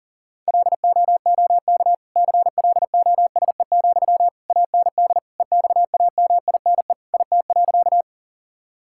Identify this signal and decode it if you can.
Morse code, heard as a keyed tone, about 40 wpm, 700 Hz. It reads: LOOK CLOSE, AND EXAMINE IT.
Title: Sentences from Top 1600 Words 40wpm